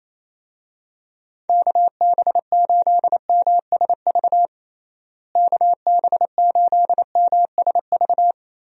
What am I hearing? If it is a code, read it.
Morse code, heard as a keyed tone, about 28 wpm, 700 Hz. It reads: KB8MSV KB8MSV
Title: USA Call Signs - 6 Characters - Repeated 2x - 28wpm